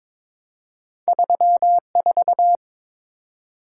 Morse code, heard as a keyed tone, about 22 wpm, 700 Hz. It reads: 34
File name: practice / 22wpm / Two Letter-Number Combinations - 22wpm